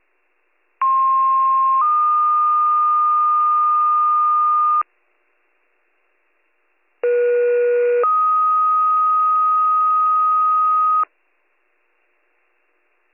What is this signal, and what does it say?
Two-tone page, heard as a one-second tone, then a three-second tone, about 1060 then 1220 Hz, about 480 then 1250 Hz. It is heard from 0.8 to 11.0 s.